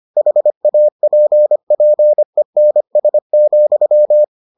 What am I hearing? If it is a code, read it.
Morse code, heard as a keyed tone, about 25 wpm, 600 Hz. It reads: HAPPENS,